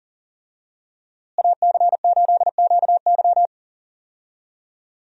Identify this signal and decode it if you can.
Morse code, heard as a keyed tone, about 37 wpm, 700 Hz. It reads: AC8QY
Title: USA Call Signs - 5 Characters - Repeated 2x - 40wpm